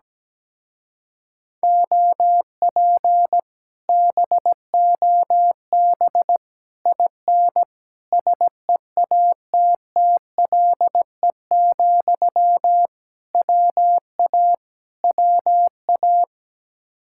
Morse code, heard as a keyed tone, about 17 wpm, 700 Hz. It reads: OP BOB IN SEATTLE, WA WA
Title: Speed-Racing Common QSO Phrases Encore - 17wpm